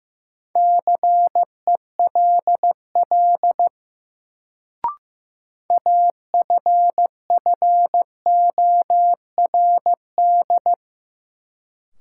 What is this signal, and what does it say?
Morse code, heard as a keyed tone, about 15 wpm, 700 Hz. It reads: CELL AFFORD
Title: Sentences from Top 1100 Words - New Word Drill - 15wpm